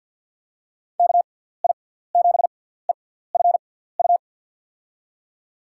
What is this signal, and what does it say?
Morse code, heard as a keyed tone, about 48 wpm, spaced as for 17 wpm, 690 Hz. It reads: KI7EFU
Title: USA Call Signs - 6x Character Spacing - 50wpm